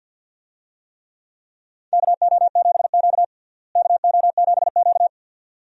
Morse code, heard as a keyed tone, about 53 wpm, 700 Hz. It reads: KO7Q KO7Q
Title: USA Call Signs - 4 Characters - Repeated 2x - 50wpm